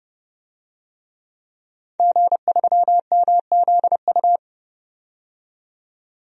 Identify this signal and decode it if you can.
Morse code, heard as a keyed tone, about 30 wpm, 700 Hz. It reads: G3MZU
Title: DXCC-100 Call Signs - 5 Characters - 30wpm